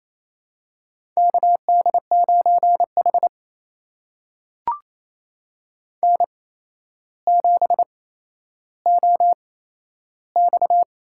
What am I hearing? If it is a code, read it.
Morse code, heard as a keyed tone, about 28 wpm, spaced as for 12 wpm, 700 Hz. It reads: KD9H N7OX
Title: USA Call Signs - 4 Characters - 8x Character Spacing - 28wpm